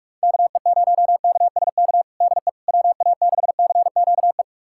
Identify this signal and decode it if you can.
Morse code, heard as a keyed tone, about 46 wpm, 700 Hz. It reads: KE0KSK DE WA6CQE
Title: USA Callsign DE Callsign 45wpm